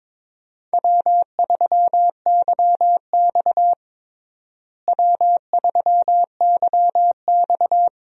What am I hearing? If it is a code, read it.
Morse code, heard as a keyed tone, about 22 wpm, 700 Hz. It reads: W3YX W3YX
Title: USA Call Signs - 4 Characters - Repeated 3x - 22wpm